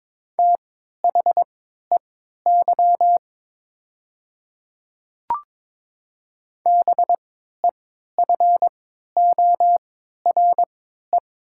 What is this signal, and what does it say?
Morse code, heard as a keyed tone, about 22 wpm, spaced as for 12 wpm, 700 Hz. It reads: THEY BEFORE
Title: 3x Character Spacing - Top 200 Words - 22wpm